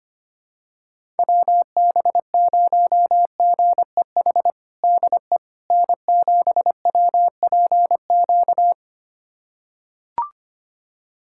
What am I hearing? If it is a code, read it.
Morse code, heard as a keyed tone, about 25 wpm, 700 Hz. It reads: WB0GEH DE N7WPQ
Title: USA Callsign DE Callsign 25wpm